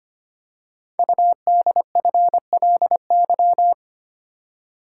Morse code, heard as a keyed tone, about 25 wpm, 700 Hz. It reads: UDFLY
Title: Code Groups - 5 Letters - 25wpm